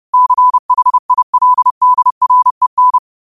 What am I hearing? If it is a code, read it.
Morse code, heard as a keyed tone, about 30 wpm, 1000 Hz. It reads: CHILDREN
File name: qotdfiles / 30wpm